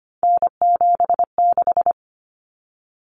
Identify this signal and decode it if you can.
Morse code, heard as a keyed tone, about 25 wpm, 700 Hz. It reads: N76
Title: Common Cut Numbers - 3-Digit 25wpm